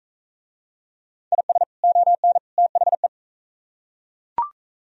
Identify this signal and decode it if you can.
Morse code, heard as a keyed tone, about 40 wpm, 700 Hz. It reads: IS ON THE